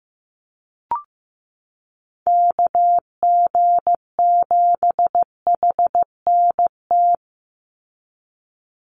Morse code, heard as a keyed tone, about 15 wpm, 700 Hz. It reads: KG7HNT
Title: USA Call Signs 15wpm